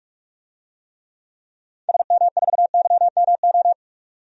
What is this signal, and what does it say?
Morse code, heard as a keyed tone, about 43 wpm, 700 Hz. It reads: SM4YMO
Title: DXCC-100 Call Signs - 6 Characters - 45wpm